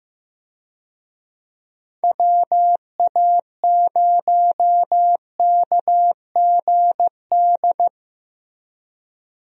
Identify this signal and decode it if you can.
Morse code, heard as a keyed tone, about 15 wpm, 700 Hz. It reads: WA0KGD
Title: USA Call Signs - 6 Characters - Repeated 2x - 15wpm